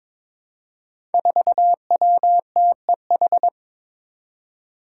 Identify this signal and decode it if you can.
Morse code, heard as a keyed tone, about 22 wpm, 700 Hz. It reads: 4WTEH
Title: Five Challenging Letter-Number Combinations 22wpm